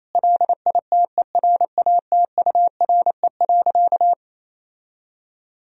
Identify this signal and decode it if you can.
Morse code, heard as a keyed tone, about 28 wpm, 700 Hz. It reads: LITERATURE.
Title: Sentences from Top 1000 Words - Review Part 3 - 28wpm